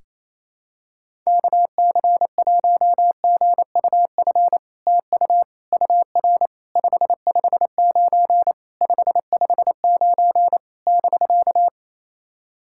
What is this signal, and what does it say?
Morse code, heard as a keyed tone, about 28 wpm, 700 Hz. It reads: KC1GUF TU UR 559 559 <BK>
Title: SOTA QSO Practice 28wpm